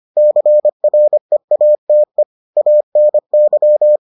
Morse code, heard as a keyed tone, about 25 wpm, 600 Hz. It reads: CREATE ANY